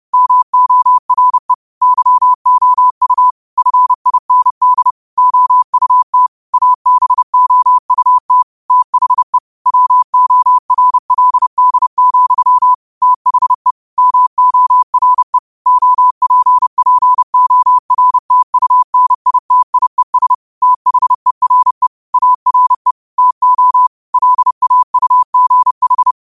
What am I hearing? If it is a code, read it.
Morse code, heard as a keyed tone, about 31 wpm, 1000 Hz. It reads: MORE YOU FIND OUT ABOUT THE WORLD, THE MORE OPPORTUNITIES THERE ARE TO LAUGH
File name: qotdfiles / 30wpm